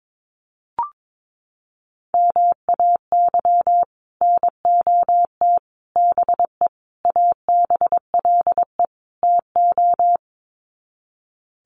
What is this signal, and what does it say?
Morse code, heard as a keyed tone, about 22 wpm, 700 Hz. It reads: MAY NOT BE ABLE TO
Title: Sets of 5 Words 22wpm